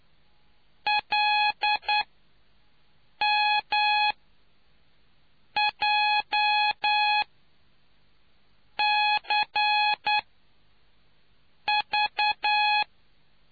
Morse code, heard as a keyed tone, about 9 wpm, spaced as for 5 wpm, 840 Hz. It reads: LMJCV